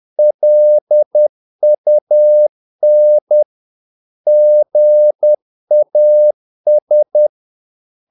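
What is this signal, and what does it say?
Morse code, heard as a keyed tone, about 10 wpm, 600 Hz. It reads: LUN GAS
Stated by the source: Morse practice of short Danish words at 10 wpm, from CW med Gnister.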